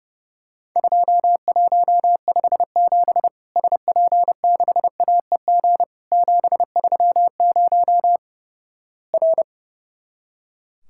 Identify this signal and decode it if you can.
Morse code, heard as a keyed tone, about 32 wpm, 700 Hz. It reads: 2157 SP6AEG 730 R
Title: Worked All Europe DX Contest - QTC Exchange - 30wpm